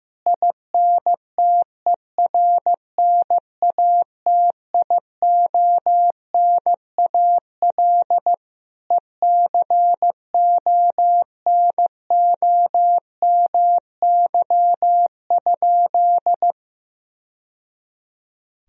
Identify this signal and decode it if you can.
Morse code, heard as a keyed tone, about 15 wpm, 700 Hz. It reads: INTERNATIONAL ECONOMY?
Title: Sentences from Top 1000 Words - Review Part 4 - 15wpm